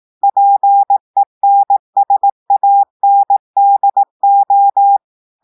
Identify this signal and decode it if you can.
Morse code, heard as a keyed tone, about 18 wpm, 800 Hz. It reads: PENSANDO
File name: portugais texte en clair 2 vitesse 18 mots minutes